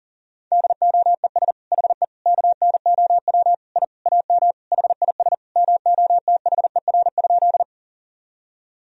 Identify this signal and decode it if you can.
Morse code, heard as a keyed tone, about 37 wpm, 700 Hz. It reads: DOES HE KNOW I AM HIS MOTHER?
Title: Sentences from Top 1000 Words - Review Part 2 - 40wpm